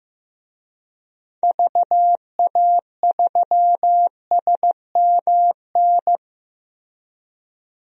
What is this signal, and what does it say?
Morse code, heard as a keyed tone, about 15 wpm, 700 Hz. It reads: VA3SMN
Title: DXCC-100 Call Signs - 6 Characters - Repeated 2x - 15wpm